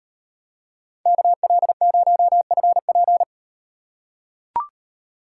Morse code, heard as a keyed tone, about 38 wpm, 700 Hz. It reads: KL0FP